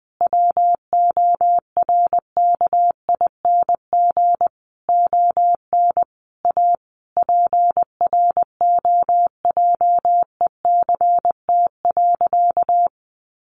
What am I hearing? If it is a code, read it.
Morse code, heard as a keyed tone, about 19 wpm, 700 Hz. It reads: WORKING ON A PROJECT.
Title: Sentences from Top 1400 Words 20wpm